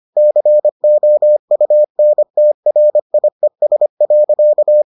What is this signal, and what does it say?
Morse code, heard as a keyed tone, about 25 wpm, 600 Hz. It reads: COUNTRIES.